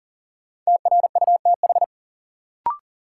Morse code, heard as a keyed tone, about 40 wpm, 700 Hz. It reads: TRUTH
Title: Sentences from Top 800 Words - New Word Drill - 40wpm